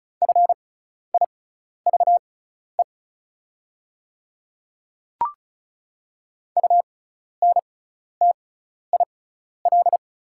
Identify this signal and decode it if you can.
Morse code, heard as a keyed tone, about 34 wpm, spaced as for 12 wpm, 690 Hz. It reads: FIVE UNTIL
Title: 6x Character Spacing - Top 500 Words - 35wpm